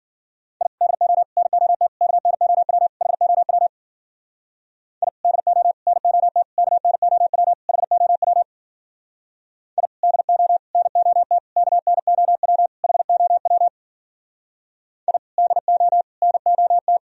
Morse code, heard as a keyed tone, about 52 wpm, 700 Hz. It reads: I DO NOT KNOW HOW I DO NOT KNOW HOW I DO NOT KNOW HOW I DO NOT